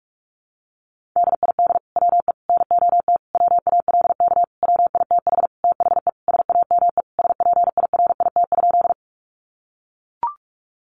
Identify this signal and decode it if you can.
Morse code, heard as a keyed tone, about 44 wpm, 700 Hz. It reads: DID WE NOT WALK WITH THE SAME SPIRIT?
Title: Sentences from Top 1400 Words 45wpm